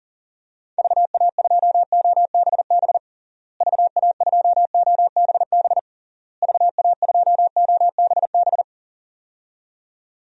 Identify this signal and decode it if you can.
Morse code, heard as a keyed tone, about 40 wpm, 700 Hz. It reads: VA2OBB VA2OBB VA2OBB
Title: DXCC-100 Call Signs - 6 Characters - Repeated 3x - 40wpm